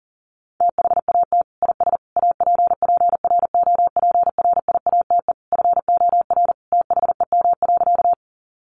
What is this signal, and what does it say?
Morse code, heard as a keyed tone, about 40 wpm, 700 Hz. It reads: THAT IS APPROPRIATE FOR THEM.
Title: Sentences from Top 900 Words 40wpm